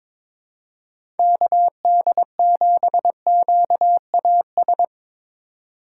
Morse code, heard as a keyed tone, about 22 wpm, 700 Hz. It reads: KD7QAS
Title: USA Call Signs - 6 Characters - 4x Character Spacing - 22wpm